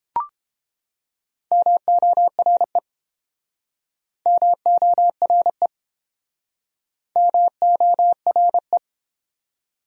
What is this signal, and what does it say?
Morse code, heard as a keyed tone, about 31 wpm, 700 Hz. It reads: MORE MORE MORE